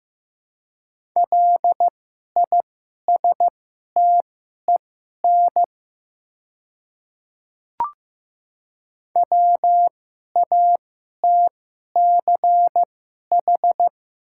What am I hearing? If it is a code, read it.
Morse code, heard as a keyed tone, about 15 wpm, spaced as for 11 wpm, 690 Hz. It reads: LISTEN WATCH
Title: 2x Character Spacing - Top 500 Words - 15wpm